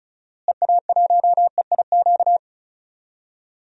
Morse code, heard as a keyed tone, about 34 wpm, 700 Hz. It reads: EA1EIQ
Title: DXCC-100 Call Signs - 6 Characters - Repeated 3x - 35wpm